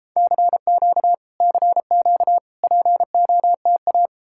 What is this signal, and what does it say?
Morse code, heard as a keyed tone, about 39 wpm, 700 Hz. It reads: CQ CQ POTA